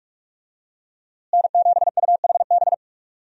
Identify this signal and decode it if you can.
Morse code, heard as a keyed tone, about 46 wpm, 700 Hz. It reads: N7UHB
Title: USA Call Signs - 5 Characters - Repeated 3x - 45wpm